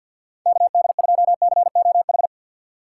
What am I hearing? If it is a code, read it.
Morse code, heard as a keyed tone, about 52 wpm, 700 Hz. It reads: KD2COH
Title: USA Call Signs - 6 Characters - 8x Character Spacing - 50wpm